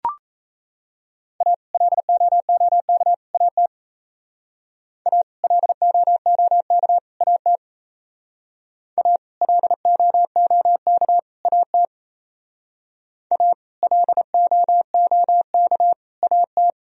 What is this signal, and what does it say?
Morse code, heard as a keyed tone, about 36 wpm, 700 Hz. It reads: A LOOK AT A LOOK AT A LOOK AT A LOOK AT